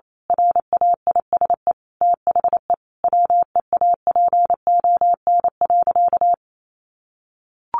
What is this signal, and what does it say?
Morse code, heard as a keyed tone, about 28 wpm, 700 Hz. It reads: RAISE THE WEAPON.
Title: A Sound Of Thunder 28wpm